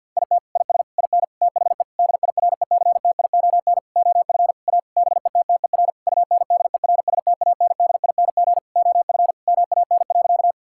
Morse code, heard as a keyed tone, about 50 wpm, 700 Hz. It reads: IT IS IN THE DIRECTION OF A BETTER UNDERSTANDING OF MAN.
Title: Sentences from Top 1000 Words - Review Part 5 - 50wpm